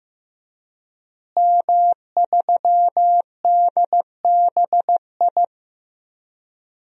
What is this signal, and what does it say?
Morse code, heard as a keyed tone, about 16 wpm, 700 Hz. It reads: M3DBI
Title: DXCC-100 Call Signs - 5 Characters - Repeated 2x - 15wpm